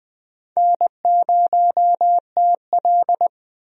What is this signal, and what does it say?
Morse code, heard as a keyed tone, about 20 wpm, 700 Hz. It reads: N0TL
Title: USA Call Signs - 4 Characters - 20wpm